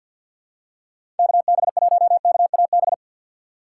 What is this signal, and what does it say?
Morse code, heard as a keyed tone, about 48 wpm, 700 Hz. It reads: KB1KAB
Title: USA Call Signs - 6 Characters - 6x Character Spacing - 50wpm